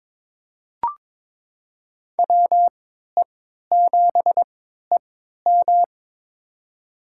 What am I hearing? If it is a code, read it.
Morse code, heard as a keyed tone, about 22 wpm, spaced as for 13 wpm, 700 Hz. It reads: WE7EM
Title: USA Call Signs - 5 Characters - 3x Character Spacing - 22wpm